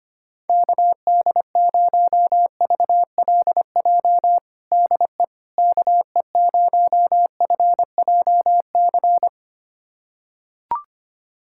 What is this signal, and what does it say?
Morse code, heard as a keyed tone, about 25 wpm, 700 Hz. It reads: KD0VLJ DE KE0FJC
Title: USA Callsign DE Callsign - 6 Characters - 25wpm